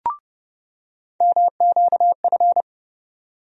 Morse code, heard as a keyed tone, about 30 wpm, 700 Hz. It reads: MQF
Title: USA Call Sign Postfixes - 3 Letters 30wpm